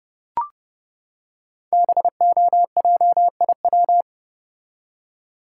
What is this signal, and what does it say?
Morse code, heard as a keyed tone, about 30 wpm, 700 Hz. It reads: BOJIW
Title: Code Groups - 5 Letters - 30wpm